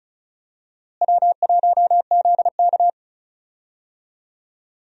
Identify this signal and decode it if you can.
Morse code, heard as a keyed tone, about 34 wpm, 700 Hz. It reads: W1ZK
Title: USA Call Signs - 4 Characters - Repeated 3x - 35wpm